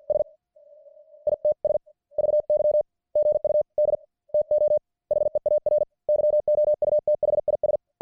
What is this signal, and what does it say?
Morse code, heard as a keyed tone, about 50 wpm, 600 Hz. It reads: S ITS VY GUD TO HEAR YOU THIS